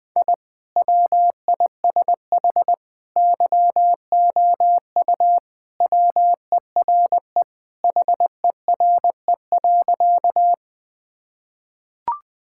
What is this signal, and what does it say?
Morse code, heard as a keyed tone, about 20 wpm, 700 Hz. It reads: I WISH YOU WERE HERE.
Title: Sentences with 2-Word Phrases - Spoken Beginning - 20wpm